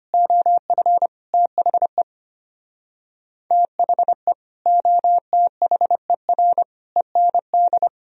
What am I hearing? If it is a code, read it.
Morse code, heard as a keyed tone, about 33 wpm, 700 Hz. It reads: OF THE THE OTHER END